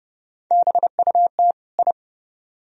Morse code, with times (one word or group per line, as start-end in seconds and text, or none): 0.5-1.5 BUT
1.8-1.9 I